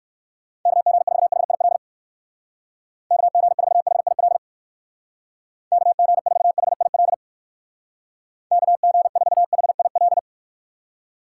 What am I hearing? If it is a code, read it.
Morse code, heard as a keyed tone, about 55 wpm, 700 Hz. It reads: KG4HIL KG4HIL KG4HIL KG4HIL
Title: Speed Racing - USA Call Signs 45wpm